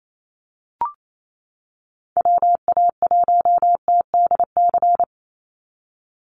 Morse code, heard as a keyed tone, about 28 wpm, 700 Hz. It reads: WA1TDC
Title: USA Call Signs 28wpm